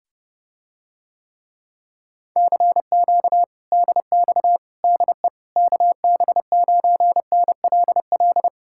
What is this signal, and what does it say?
Morse code, heard as a keyed tone, about 30 wpm, 700 Hz. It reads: CQ DX DE KB9NLL